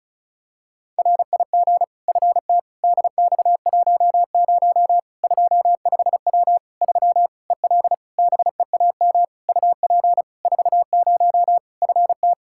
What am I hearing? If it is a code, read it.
Morse code, heard as a keyed tone, about 36 wpm, 700 Hz. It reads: RIG FT DX10 25W 3 EL BEAM UP 40 FT
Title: Practice QSO - Exchange 2 - Info - 35wpm